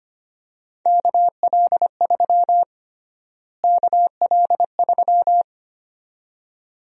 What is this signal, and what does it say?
Morse code, heard as a keyed tone, about 25 wpm, 700 Hz. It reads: KL3 KL3
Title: USA Call Sign Prefixes - Repeated 2x - 25wpm